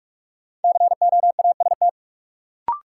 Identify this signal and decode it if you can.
Morse code, heard as a keyed tone, about 45 wpm, 700 Hz. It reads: COAST
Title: Sentences from Top 1000 Words - Review Part 4 - New Word Drill - 45wpm